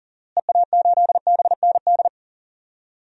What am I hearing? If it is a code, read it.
Morse code, heard as a keyed tone, about 37 wpm, 700 Hz. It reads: EA8BND
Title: DXCC-100 Call Signs - 6 Characters - Repeated 3x - 40wpm